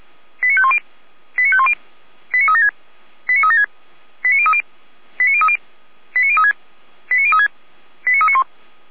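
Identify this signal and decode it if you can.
Five-tone selective call (ZVEI), heard as quick runs of five tones, tones of about 70 ms. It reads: 89510 89510 89376 89376 89030 89030 89036 89036 89391
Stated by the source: Received on an SDR, ZVEI five-tone code 89042.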